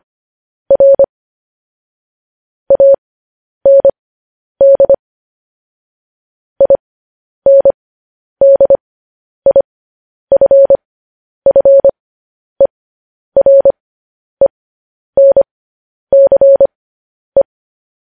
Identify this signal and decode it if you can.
Morse code, heard as a keyed tone, about 25 wpm, spaced as for 10 wpm, 550 Hz. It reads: R AND INDIFFERENCE
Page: news headlines in morse code at 10 wpm